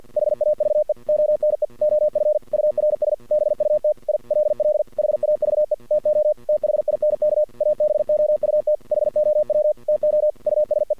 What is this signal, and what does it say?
Morse code, heard as a keyed tone, about 50 wpm, 610 Hz. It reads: HAVE ONE OF URS BUT I LL HAVE TO THINK ABOUT HOW TO USE
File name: archived / CRSnet